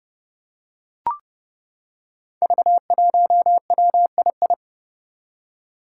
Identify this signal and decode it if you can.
Morse code, heard as a keyed tone, about 28 wpm, 700 Hz. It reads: V1WII
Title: Five Challenging Letter-Number Combinations 30wpm